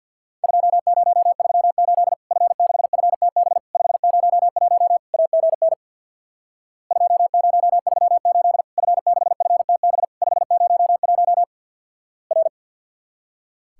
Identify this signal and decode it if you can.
Morse code, heard as a keyed tone, about 50 wpm, 700 Hz. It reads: 2038 F6FTB 501 AGN 2038 F6FTB 501 R
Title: Worked All Europe DX Contest - QTC Exchange with Repeats - 50wpm